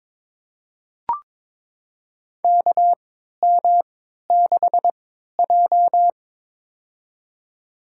Morse code, heard as a keyed tone, about 22 wpm, spaced as for 12 wpm, 700 Hz. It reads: KM6J
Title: USA Call Signs - 4 Characters - 3x Character Spacing - 22wpm